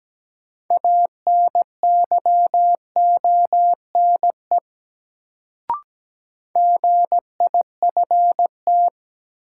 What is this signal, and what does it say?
Morse code, heard as a keyed tone, about 17 wpm, 700 Hz. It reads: ANYONE GIFT
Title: Sentences from Top 900 Words - New Word Drill - 17wpm